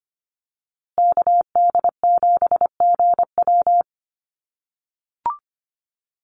Morse code, heard as a keyed tone, about 25 wpm, 700 Hz. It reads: KD7GW
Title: USA Call Signs Repeated 2x 25wpm